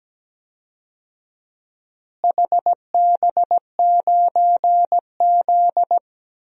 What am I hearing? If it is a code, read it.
Morse code, heard as a keyed tone, about 17 wpm, 700 Hz. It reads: HB9Z
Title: DXCC-100 Call Signs - 4 Characters - Repeated 3x - 17wpm